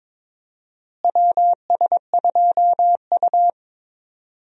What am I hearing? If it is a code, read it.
Morse code, heard as a keyed tone, about 22 wpm, 700 Hz. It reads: WS2U
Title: USA Call Signs - 4 Characters - Repeated 2x - 22wpm